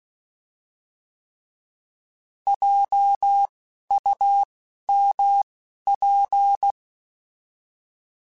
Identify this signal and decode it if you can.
Morse code, heard as a keyed tone, about 16 wpm, 800 Hz. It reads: J U M P